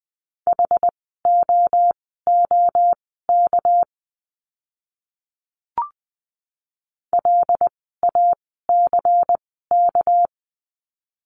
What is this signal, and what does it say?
Morse code, heard as a keyed tone, about 20 wpm, spaced as for 14 wpm, 700 Hz. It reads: HOOK LACK
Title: 199 Lesson 32f - Letter k - Words 20wpm